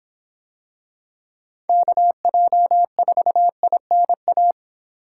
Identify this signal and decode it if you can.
Morse code, heard as a keyed tone, about 25 wpm, 700 Hz. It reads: KJ4INA